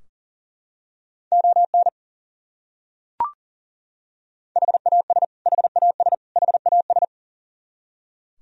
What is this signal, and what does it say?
Morse code, heard as a keyed tone, about 40 wpm, 700 Hz. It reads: ON HAS HAS HAS
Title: Top 100 Words Repeated 40wpm